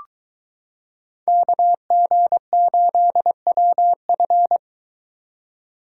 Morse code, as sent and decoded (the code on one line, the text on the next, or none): -.- --. ---.. .-- ..-.
KG8WF